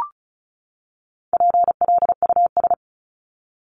Morse code, heard as a keyed tone, about 34 wpm, 700 Hz. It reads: PLUS
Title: Top 1001-1250 Words 35wpm